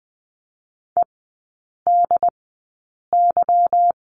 Morse code, heard as a keyed tone, about 20 wpm, spaced as for 8 wpm, 690 Hz. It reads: EDY
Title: 145 Lesson 24e - Letter y - Three Characters 20wpm